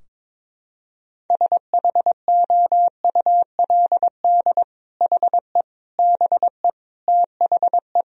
Morse code, heard as a keyed tone, about 22 wpm, 700 Hz. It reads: SHOULD HE BE THE
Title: Sentences from Top 1000 Words - Review Part 4 - 22wpm